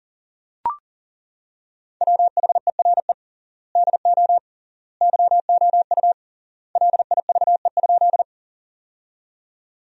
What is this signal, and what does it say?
Morse code, heard as a keyed tone, about 37 wpm, 700 Hz. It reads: WHERE DO YOU LIVE?
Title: 3x Word-Spacing - Sentences from Top 200 Words - 40wpm